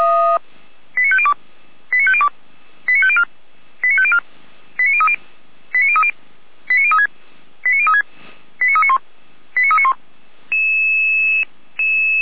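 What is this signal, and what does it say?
Five-tone selective call (ZVEI), heard as quick runs of five tones, tones of about 80 ms. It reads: 89502 89502 89504 89504 89030 89030 89036 89036 89391 89391
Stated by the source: ZVEI five-tone selective call 89062, received via SDR.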